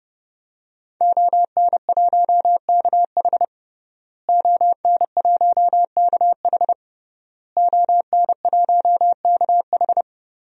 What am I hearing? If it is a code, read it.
Morse code, heard as a keyed tone, about 30 wpm, 700 Hz. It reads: ON1KH ON1KH ON1KH
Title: DXCC-100 Call Signs - 5 Characters - Repeated 3x - 30wpm